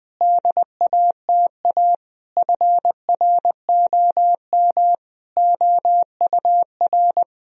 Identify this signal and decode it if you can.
Morse code, heard as a keyed tone, about 20 wpm, 700 Hz. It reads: DATA FROM OUR
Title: Sentences from Top 1000 Words - Review Part 3 - 20wpm